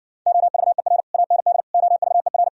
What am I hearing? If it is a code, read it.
Morse code, heard as a keyed tone, about 69 wpm, 700 Hz. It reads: OVER AND OVER